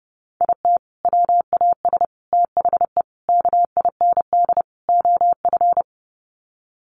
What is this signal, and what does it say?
Morse code, heard as a keyed tone, about 30 wpm, 700 Hz. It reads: IT WAS THE KIND OF
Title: Sets of 5 Words Encore2 30wpm